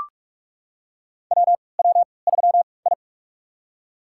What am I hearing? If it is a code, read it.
Morse code, heard as a keyed tone, about 44 wpm, spaced as for 25 wpm, 700 Hz. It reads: WW3I
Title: USA Call Signs - 3x Character Spacing - 45wpm